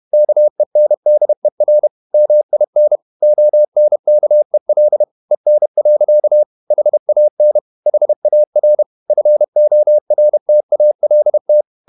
Morse code, heard as a keyed tone, about 31 wpm, 600 Hz. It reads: KENDER MIN ONKEL EN. HAN HAR FORTALT
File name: Ferie QRQ 31wpm